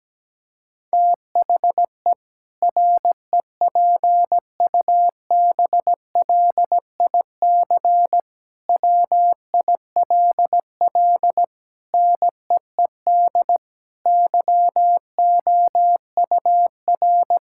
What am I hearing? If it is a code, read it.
Morse code, heard as a keyed tone, about 17 wpm, 700 Hz. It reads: THE REPUBLIC WILL NEED YOUR
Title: Sentences from Top 1600 Words 17wpm